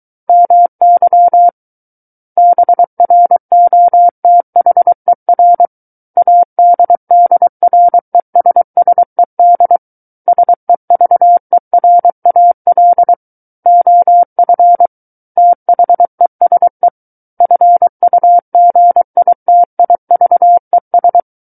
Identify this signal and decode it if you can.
Morse code, heard as a keyed tone, about 23 wpm, 700 Hz. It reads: MY BROTHER ADDRESSED SEVERAL OF THESE FUGITIVES
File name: Book One / 14-Chapter 14 - 23 WPM